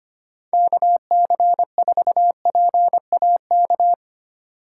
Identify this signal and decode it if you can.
Morse code, heard as a keyed tone, about 25 wpm, 700 Hz. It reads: KC4PAK
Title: USA Call Signs - 6 Characters - 8x Character Spacing - 25wpm